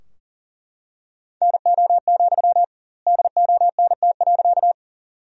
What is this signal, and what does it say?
Morse code, heard as a keyed tone, about 39 wpm, 700 Hz. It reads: NO, DONT.
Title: 2x Word-Spacing - Sentences from Top 300 Words - 40wpm